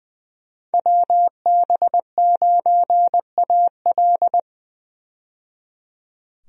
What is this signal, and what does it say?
Morse code, heard as a keyed tone, about 20 wpm, 700 Hz. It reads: WB9AL
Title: USA Call Signs - 5 Characters - 20wpm